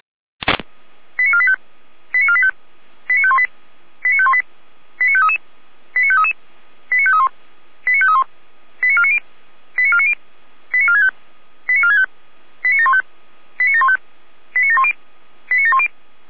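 Five-tone selective call (ZVEI), heard as quick runs of five tones, tones of about 70 ms. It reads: 89485 89485 89518 89518 89533 89533 89531 89531 89490 89490 89465 89465 89715 89715 89710 89710
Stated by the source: ZVEI five-tone selective call received on an SDR.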